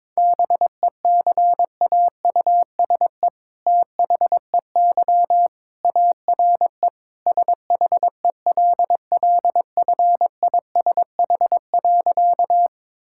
Morse code, heard as a keyed tone, about 22 wpm, 700 Hz. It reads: BECAUSE THEY ARE SHELLFISH.
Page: morse-code-ninja-practice-22wpm